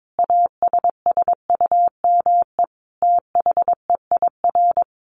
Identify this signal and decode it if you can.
Morse code, heard as a keyed tone, about 22 wpm, 700 Hz. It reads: ASSUME THEIR
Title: Sentences from Top 1500 Words 22wpm